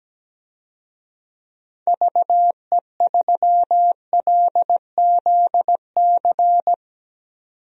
Morse code, heard as a keyed tone, about 17 wpm, 700 Hz. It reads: VE3LZC
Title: DXCC-100 Call Signs - 6 Characters - 17wpm